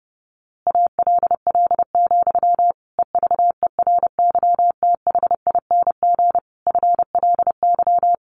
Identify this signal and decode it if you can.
Morse code, heard as a keyed tone, about 30 wpm, 700 Hz. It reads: ALL, EVERYTHING FLY
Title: A Sound Of Thunder 30wpm